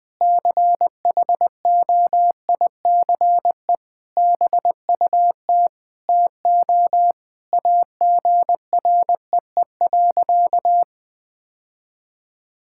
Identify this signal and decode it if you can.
Morse code, heard as a keyed tone, about 20 wpm, 700 Hz. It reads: CHOICE BUT TO AGREE.
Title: Sentences with 5-Word Phrases - 20wpm